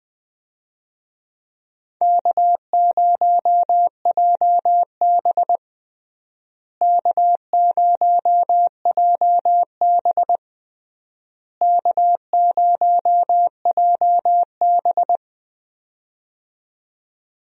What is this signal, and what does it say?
Morse code, heard as a keyed tone, about 20 wpm, 700 Hz. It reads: K0JB K0JB K0JB
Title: USA Call Signs - 4 Characters - Repeated 3x - 20wpm